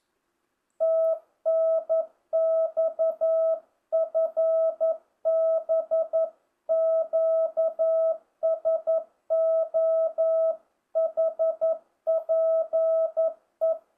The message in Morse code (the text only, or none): TNX FB QSO HPE